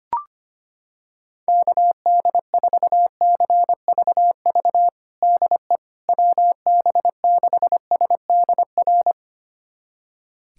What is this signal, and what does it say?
Morse code, heard as a keyed tone, about 25 wpm, 700 Hz. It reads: KD4CVV DE WB6SDR
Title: USA Callsign DE Callsign - 6 Characters - 25wpm